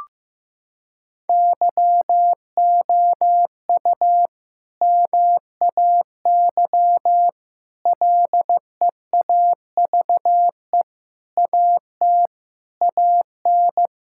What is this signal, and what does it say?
Morse code, heard as a keyed tone, about 15 wpm, 700 Hz. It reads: YOU MAY LEAVE AT AN
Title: Sentences from Top 1000 Words - Review Part 5 - 15wpm